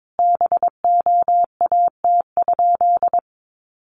Morse code, heard as a keyed tone, about 22 wpm, 700 Hz. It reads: BOAT?
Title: Sentences from Top 1300 Words 22wpm